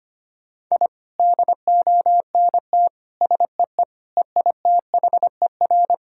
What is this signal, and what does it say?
Morse code, heard as a keyed tone, about 25 wpm, 700 Hz. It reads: I DONT SEE EITHER
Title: Sentences from Top 1000 Words - Review Part 4 - 25wpm